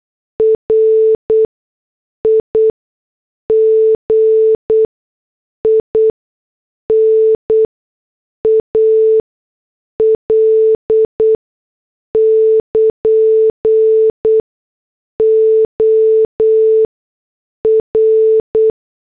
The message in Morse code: RIGINAL <KN> OR